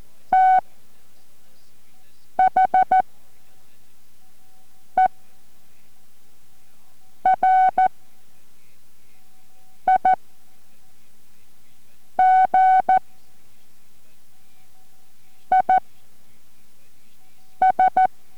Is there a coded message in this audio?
Yes: Morse code from 0.3 to 18.1 s, heard as a keyed tone, about 14 wpm, spaced as for 4 wpm, 750 Hz. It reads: THE RIG IS